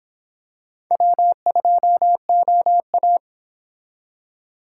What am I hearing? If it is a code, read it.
Morse code, heard as a keyed tone, about 25 wpm, 700 Hz. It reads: W2OA